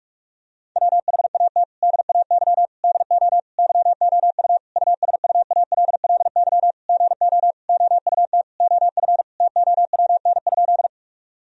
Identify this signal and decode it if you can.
Morse code, heard as a keyed tone, about 46 wpm, 700 Hz. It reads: WHAT DAY DO YOU USUALLY GO OUT OF TOWN?
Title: Sentences from Top 1000 Words - Review Part 3 - 45wpm